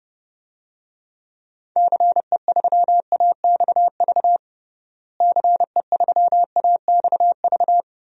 Morse code, heard as a keyed tone, about 32 wpm, 700 Hz. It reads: CE3AXV CE3AXV
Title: DXCC-100 Call Signs - 6 Characters - Repeated 2x - 30wpm